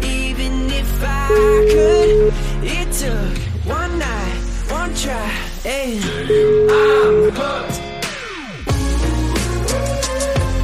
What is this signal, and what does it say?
Ringback tone (Europe, 425 Hz), with speech.